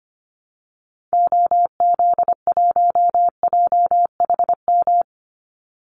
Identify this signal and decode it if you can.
Morse code, heard as a keyed tone, about 25 wpm, 700 Hz. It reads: OZ1JHM
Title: Call Signs Repeated 2x - 25wpm